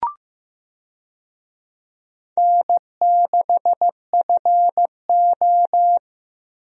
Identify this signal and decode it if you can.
Morse code, heard as a keyed tone, about 15 wpm, 700 Hz. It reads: N6FO
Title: USA Call Signs - 4 Characters - Repeated 2x - 15wpm